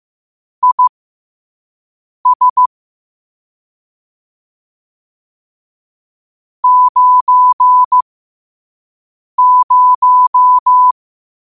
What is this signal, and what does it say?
Morse code, heard as a keyed tone, about 15 wpm, spaced as for 5 wpm, 1000 Hz. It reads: IS 90
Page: Quotes for Thu, 14 Aug 2025 in Morse Code at 5 words per minute.